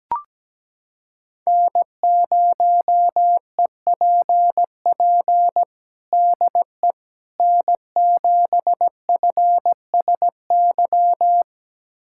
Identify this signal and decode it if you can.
Morse code, heard as a keyed tone, about 17 wpm, 700 Hz. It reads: N0EPP DE N7FSY
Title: USA Callsign DE Callsign - 5 Characters - 17wpm